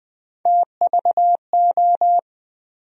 Morse code, heard as a keyed tone, about 20 wpm, 700 Hz. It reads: TVO
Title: USA Call Sign Postfixes - 3 Letters 20wpm